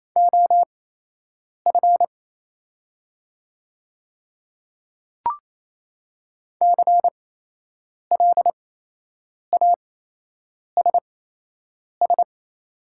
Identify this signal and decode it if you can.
Morse code, heard as a keyed tone, about 28 wpm, spaced as for 8 wpm, 700 Hz. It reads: OF CLASS
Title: 8x Character Spacing - Top 500 Words - 28wpm